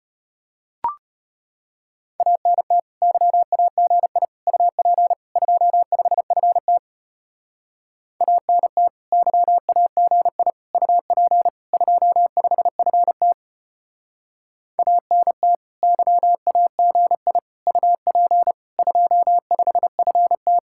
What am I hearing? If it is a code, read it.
Morse code, heard as a keyed tone, about 31 wpm, 700 Hz. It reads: ANT YAGI UP 25FT ANT YAGI UP 25FT ANT YAGI UP 25FT
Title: Speed-Racing Common QSO Phrases - 25wpm